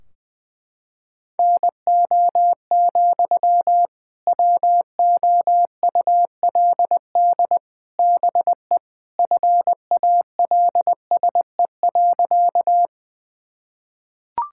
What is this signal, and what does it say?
Morse code, heard as a keyed tone, about 20 wpm, 700 Hz. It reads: NO, WOULD BE FALSE.
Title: Sentences from Top 1600 Words 20wpm